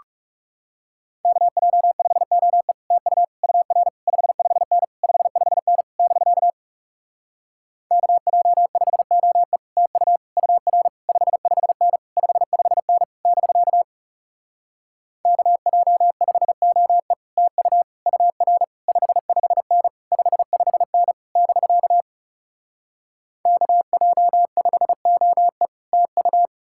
Morse code, heard as a keyed tone, about 40 wpm, 700 Hz. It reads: KJ5OE TU UR 55N 55N <BK> KJ5OE TU UR 55N 55N <BK> KJ5OE TU UR 55N 55N <BK> KJ5OE TU